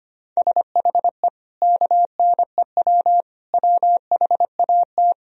Morse code, heard as a keyed tone, about 25 wpm, 700 Hz. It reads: SHE KNEW WHAT
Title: Sentences from Top 400 Words and Abbreviations 25wpm